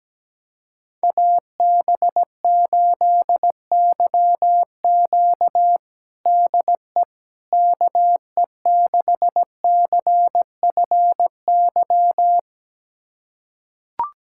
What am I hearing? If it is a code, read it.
Morse code, heard as a keyed tone, about 17 wpm, 700 Hz. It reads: AB8YQ DE KE6CFY
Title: USA Callsign DE Callsign 17wpm